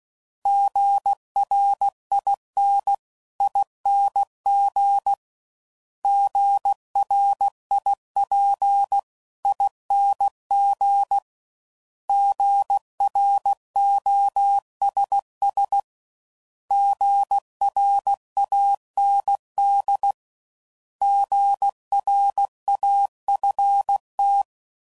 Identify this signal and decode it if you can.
Morse code, heard as a keyed tone, about 16 wpm, 800 Hz. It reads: GRINING GRIPING GROSS GRAND GRAFT
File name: Prefixes / By First Letter / Gs